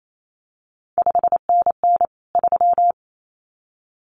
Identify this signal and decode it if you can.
Morse code, heard as a keyed tone, about 28 wpm, 700 Hz. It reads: 5NN 3
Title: CQ WW DX Contest Exchange - 28wpm